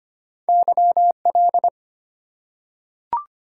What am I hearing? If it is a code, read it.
Morse code, heard as a keyed tone, about 25 wpm, 700 Hz. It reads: YL
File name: Top 90 QSO Elements - 4x Character Spacing - 25wpm